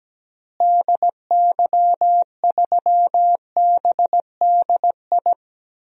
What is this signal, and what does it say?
Morse code, heard as a keyed tone, about 17 wpm, 700 Hz. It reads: DY3BDI
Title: DXCC-100 Call Signs - 6 Characters - Repeated 3x - 17wpm